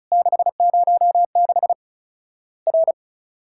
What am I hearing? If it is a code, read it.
Morse code, heard as a keyed tone, about 36 wpm, 700 Hz. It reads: 606 R
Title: Worked All Europe DX Contest - QTC Exchange - 35wpm